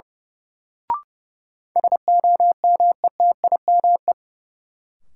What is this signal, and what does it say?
Morse code, heard as a keyed tone, about 30 wpm, 700 Hz. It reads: SOMETIME
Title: Sentences from Top 1500 Words - New Word Drill Alt - 30wpm